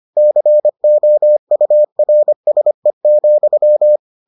Morse code, heard as a keyed tone, about 25 wpm, 600 Hz. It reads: COURSE,